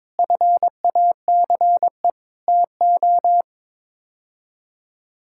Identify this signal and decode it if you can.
Morse code, heard as a keyed tone, about 22 wpm, 700 Hz. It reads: FACE TO
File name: practice / 22wpm / Sets of 6 Words - 22wpm